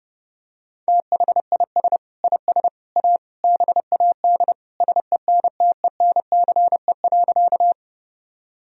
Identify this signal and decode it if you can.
Morse code, heard as a keyed tone, about 32 wpm, 700 Hz. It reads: THIS IS A BAD SENTENCE.
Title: Sentences from Top 1000 Words - Review Part 1 - 30wpm